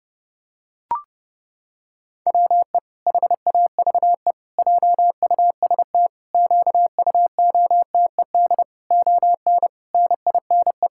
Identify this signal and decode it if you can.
Morse code, heard as a keyed tone, about 32 wpm, 700 Hz. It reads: WE HAVE JUST QUOTED ON NINE
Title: Pangram Challenge v2 - 30wpm